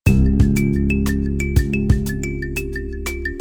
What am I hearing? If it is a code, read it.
There is no code.